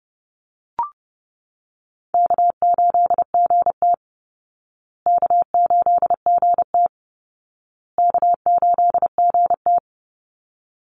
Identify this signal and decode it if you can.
Morse code, heard as a keyed tone, about 28 wpm, 700 Hz. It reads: K8GT K8GT K8GT
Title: Easy Call Signs Repeated 3x - 30wpm